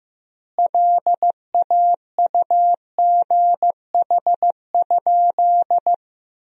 Morse code, heard as a keyed tone, about 15 wpm, 700 Hz. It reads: LAUGH?
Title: 3x Word-Spacing - Sentences from Top 500 Words - 15wpm